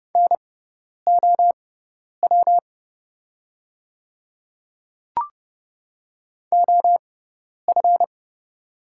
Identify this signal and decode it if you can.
Morse code, heard as a keyed tone, about 30 wpm, spaced as for 10 wpm, 690 Hz. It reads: NOW OF
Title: 6x Character Spacing - Top 100 Words - 30wpm